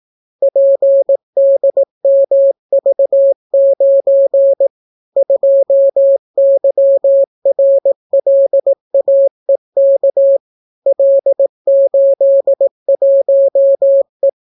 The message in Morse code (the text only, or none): PDMV9 2YRLAEK L81E